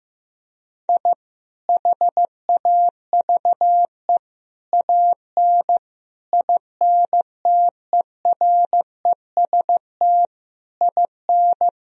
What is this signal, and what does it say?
Morse code, heard as a keyed tone, about 15 wpm, 700 Hz. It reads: I HAVE AN INTEREST IN
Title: Sentences from Top 1000 Words - Review Part 2 - 15wpm